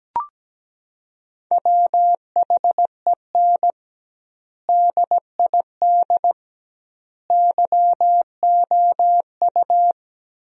Morse code, heard as a keyed tone, about 17 wpm, 700 Hz. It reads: WHEN DID YOU
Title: 2x Word-Spacing - Sentences from Top 300 Words - 17wpm